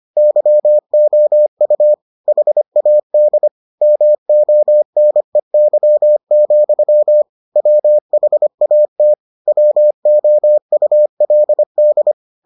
Morse code, heard as a keyed tone, about 25 wpm, 600 Hz. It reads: YOU HAD MONEY, WHAT WOULD